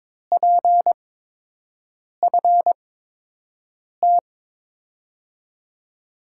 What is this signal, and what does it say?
Morse code, heard as a keyed tone, about 21 wpm, spaced as for 6 wpm, 700 Hz. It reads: PFT
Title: USA Call Signs - 8x Character Spacing - 22wpm